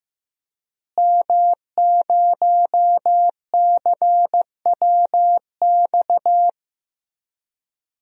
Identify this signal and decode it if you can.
Morse code, heard as a keyed tone, about 15 wpm, 700 Hz. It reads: M0CWX
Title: DXCC-100 Call Signs - 5 Characters - Repeated 3x - 15wpm